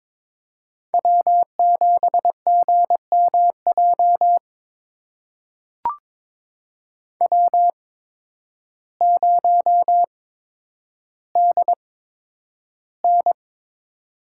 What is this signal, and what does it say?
Morse code, heard as a keyed tone, about 22 wpm, spaced as for 10 wpm, 700 Hz. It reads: W7GMJ W0DN
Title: USA Call Signs - 8x Character Spacing - 22wpm